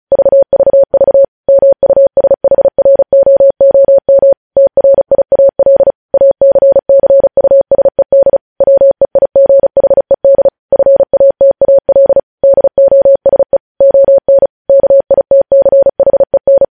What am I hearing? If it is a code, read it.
Morse code, heard as a keyed tone, about 36 wpm, 550 Hz. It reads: VVV MUSHROOM TRIAL ACCUSED WEIGHED FATAL DOSE ON KITCHEN
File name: podcast / News headlines-1x-35x35